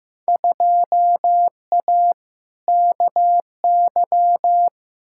Morse code, heard as a keyed tone, about 15 wpm, 700 Hz. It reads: 2A KY
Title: Field Day Exchanges - 15wpm